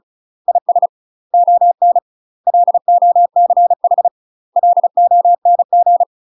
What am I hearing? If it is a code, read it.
Morse code, heard as a keyed tone, about 36 wpm, 700 Hz. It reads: IS ON LOCH LONG
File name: UK MASTER QSOs TRAINING / 35WPM / UK MASTER QSOs 0007